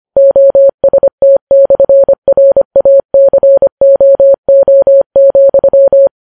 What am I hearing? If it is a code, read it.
Morse code, heard as a keyed tone, about 25 wpm, 550 Hz. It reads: OST/RACOO,